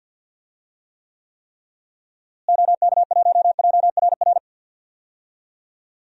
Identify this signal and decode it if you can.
Morse code, heard as a keyed tone, about 49 wpm, 700 Hz. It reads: OK1JRR
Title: DXCC-100 Call Signs - 6 Characters - Repeated 3x - 50wpm